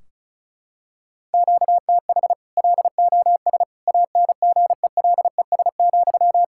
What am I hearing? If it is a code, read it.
Morse code, heard as a keyed tone, about 36 wpm, 700 Hz. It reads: QTH LOS ANGELES,
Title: Practice QSO - Exchange 1 - Beginning - 35wpm